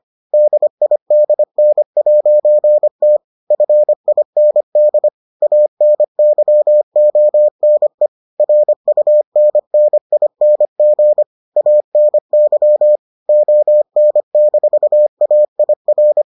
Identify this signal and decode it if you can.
Morse code, heard as a keyed tone, about 25 wpm, 600 Hz. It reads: DIDN'T FIND ANYONE RUNNING ANY ON-AIR